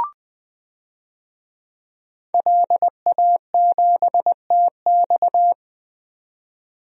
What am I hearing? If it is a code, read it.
Morse code, heard as a keyed tone, about 20 wpm, 700 Hz. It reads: LA7TX
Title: DXCC-100 Call Signs - 5 Characters - 20wpm